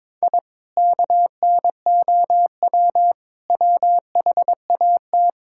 Morse code, heard as a keyed tone, about 22 wpm, 700 Hz. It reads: I KNOW WHAT
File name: practice / 22wpm / Sentences with 2-Word Phrases - 22wpm